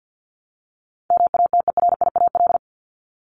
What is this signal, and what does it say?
Morse code, heard as a keyed tone, about 48 wpm, 700 Hz. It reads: MATERIAL
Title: ES:S Suffix - Top 100 Words with Base Words 50wpm